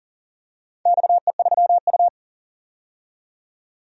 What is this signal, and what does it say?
Morse code, heard as a keyed tone, about 40 wpm, 700 Hz. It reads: XE3U
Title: DXCC-100 Call Signs - 4 Characters - 40wpm